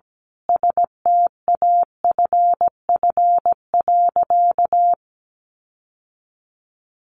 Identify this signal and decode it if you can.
Morse code, heard as a keyed tone, about 17 wpm, 700 Hz. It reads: STAFF.
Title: Sentences from Top 1500 Words 17wpm